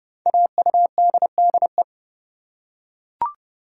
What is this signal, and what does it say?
Morse code, heard as a keyed tone, about 28 wpm, 700 Hz. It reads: AUDDE
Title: Code Groups - 5 Letters - 30wpm